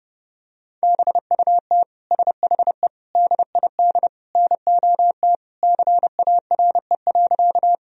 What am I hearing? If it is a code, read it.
Morse code, heard as a keyed tone, about 30 wpm, 700 Hz. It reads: BUT SHE DID NOT CARE.
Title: Sentences from Top 1000 Words - Review Part 1 - 30wpm